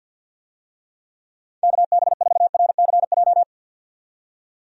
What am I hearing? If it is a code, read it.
Morse code, heard as a keyed tone, about 52 wpm, 700 Hz. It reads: KB4RCJ
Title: USA Call Signs - 6 Characters - 50wpm